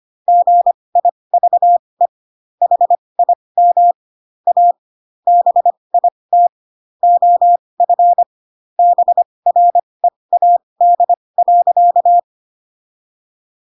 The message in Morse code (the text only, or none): GIVE HIM A BIT OF BREAD.